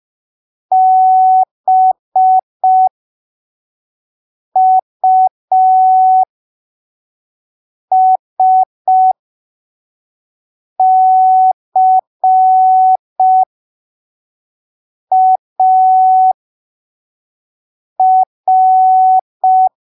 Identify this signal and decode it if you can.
Morse code, heard as a keyed tone, about 5 wpm, spaced as for 3 wpm, 750 Hz. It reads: BUSCAR
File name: arquivos / texto / texto 15 05ppm